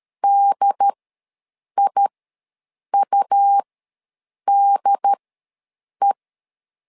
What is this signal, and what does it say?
Morse code, heard as a keyed tone, about 13 wpm, spaced as for 7 wpm, 800 Hz. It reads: DIUDE
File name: LEZIONI / 2 UHDN / 2 UHD ws 12wpm 1722